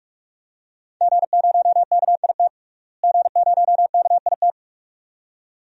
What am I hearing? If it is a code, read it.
Morse code, heard as a keyed tone, about 44 wpm, 700 Hz. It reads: G0KIT G0KIT
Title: DXCC-100 Call Signs - 5 Characters - Repeated 2x - 45wpm